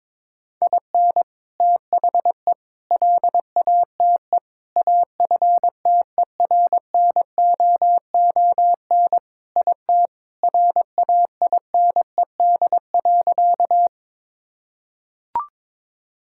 Morse code, heard as a keyed tone, about 22 wpm, 700 Hz. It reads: IN THE LATE AFTERNOON IT RAINED.
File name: practice / 22wpm / Sentences with 3-Word Phrases - 22wpm